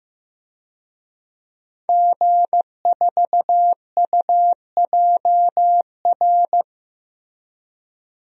Morse code, heard as a keyed tone, about 15 wpm, 700 Hz. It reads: G4UJR
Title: DXCC-100 Call Signs - 5 Characters - Repeated 2x - 15wpm